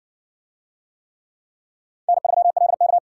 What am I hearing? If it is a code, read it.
Morse code, heard as a keyed tone, about 60 wpm, 700 Hz. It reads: N3LK